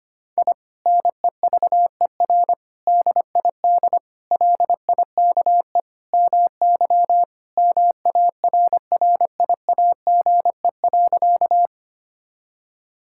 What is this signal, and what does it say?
Morse code, heard as a keyed tone, about 25 wpm, 700 Hz. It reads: I NEVER DID LIKE MY MARRIAGE.
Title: Sentences from Top 1000 Words - Review Part 5 - 25wpm